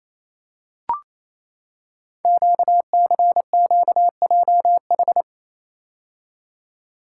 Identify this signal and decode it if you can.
Morse code, heard as a keyed tone, about 28 wpm, 700 Hz. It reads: QCQJH
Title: Code Groups - 5 Letters - 28wpm